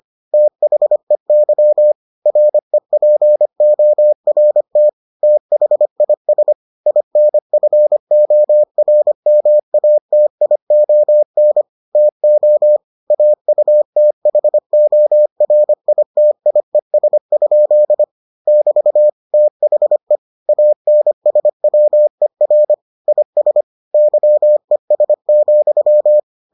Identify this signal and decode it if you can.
Morse code, heard as a keyed tone, about 25 wpm, 600 Hz. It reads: THEY REPORT THIS INFORMATION TO AUTHORITIES? <BT> THE ANSWER IS YES,